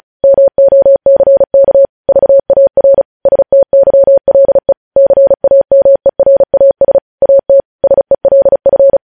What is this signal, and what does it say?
Morse code, heard as a keyed tone, about 36 wpm, 550 Hz. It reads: MOCK VAR STYLE CAMERAS AT SELF
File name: podcast / News headlines-1x-35x35